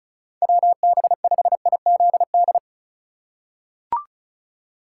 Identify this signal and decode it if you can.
Morse code, heard as a keyed tone, about 34 wpm, 700 Hz. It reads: WB5IZD
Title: USA Call Signs - 6 Characters - 2x Character Spacing - 35wpm